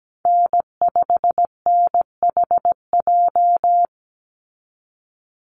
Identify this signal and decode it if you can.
Morse code, heard as a keyed tone, about 17 wpm, 700 Hz. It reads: N5NHJ
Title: Call Signs Repeated 3x - 17wpm